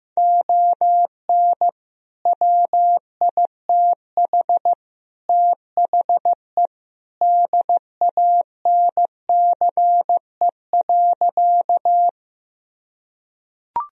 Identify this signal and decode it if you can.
Morse code, heard as a keyed tone, about 15 wpm, 700 Hz. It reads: ON WITH THE DANCE.
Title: Sentences from Top 1000 Words - Review Part 4 - 15wpm